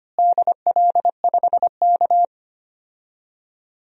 Morse code, heard as a keyed tone, about 25 wpm, 700 Hz. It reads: DL5K
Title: DXCC-100 Call Signs - 4 Characters - Repeated 2x - 25wpm